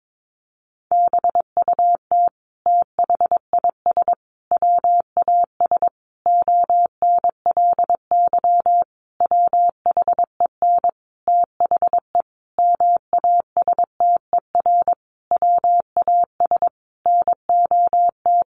Morse code, heard as a keyed tone, about 22 wpm, 700 Hz. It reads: BUT THIS WAS ONLY WHEN THE MASTER WAS NOT
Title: Sentences from Top 1300 Words 22wpm